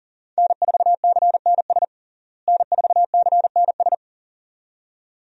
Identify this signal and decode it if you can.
Morse code, heard as a keyed tone, about 40 wpm, 700 Hz. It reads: N4CNS N4CNS
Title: USA Call Signs - 5 Characters - Repeated 2x - 40wpm